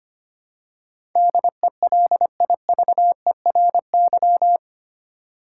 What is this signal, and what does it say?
Morse code, heard as a keyed tone, about 25 wpm, 700 Hz. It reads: DELIVERY
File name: Eight Letter Words - 25wpm